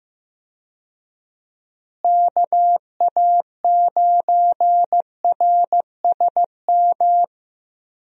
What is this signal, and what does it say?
Morse code, heard as a keyed tone, about 15 wpm, 700 Hz. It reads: KA9RSM
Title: USA Call Signs - 6 Characters - Repeated 3x - 15wpm